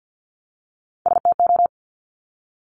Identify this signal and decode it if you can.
Morse code, heard as a keyed tone, about 50 wpm, 700 Hz. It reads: STO
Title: Top Three Letter Combinations in Words 50wpm